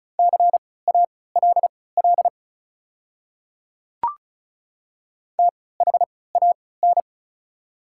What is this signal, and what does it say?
Morse code, heard as a keyed tone, about 36 wpm, spaced as for 20 wpm, 690 Hz. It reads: CALL THAN
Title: 3x Character Spacing - Top 100 Words - 35wpm